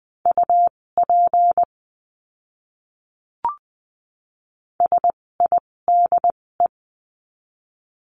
Morse code, heard as a keyed tone, about 20 wpm, spaced as for 16 wpm, 700 Hz. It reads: UP SIDE
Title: 256 Lesson 42a - Words 12-20wpm